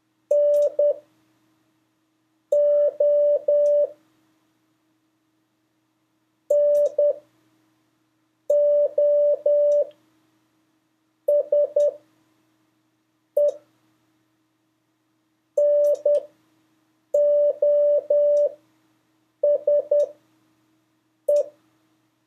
Morse code, heard as a keyed tone, about 10 wpm, spaced as for 5 wpm, 580 Hz. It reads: NO NOSE NOSE
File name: 2of5-words-1